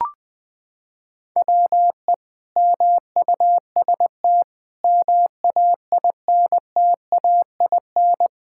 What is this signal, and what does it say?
Morse code, heard as a keyed tone, about 20 wpm, 700 Hz. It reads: WE MUST MAINTAIN
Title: Sentences from Top 1000 Words - Review Part 5 - 20wpm